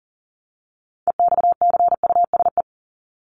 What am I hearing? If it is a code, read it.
Morse code, heard as a keyed tone, about 40 wpm, 700 Hz. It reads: EXCUSE
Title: Top 501-750 Words 40wpm